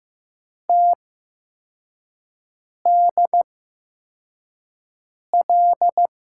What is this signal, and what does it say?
Morse code, heard as a keyed tone, about 15 wpm, spaced as for 4 wpm, 700 Hz. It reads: TDL
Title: USA Call Signs - 6 Characters - 8x Character Spacing - 15wpm